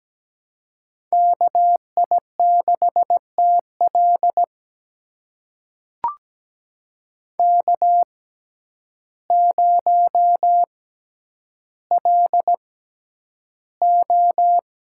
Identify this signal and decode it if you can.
Morse code, heard as a keyed tone, about 17 wpm, spaced as for 9 wpm, 700 Hz. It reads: KI6TL K0LO
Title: USA Call Signs - 6x Character Spacing - 17wpm